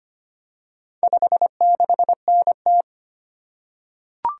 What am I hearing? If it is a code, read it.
Morse code, heard as a keyed tone, about 25 wpm, 700 Hz. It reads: 56NT
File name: practice / 25wpm / Four Letter-Number Combinations - 25wpm